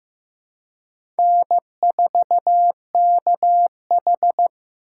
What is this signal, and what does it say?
Morse code, heard as a keyed tone, about 15 wpm, 700 Hz. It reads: N4KH
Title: K1USN SST QSO Practice - 15wpm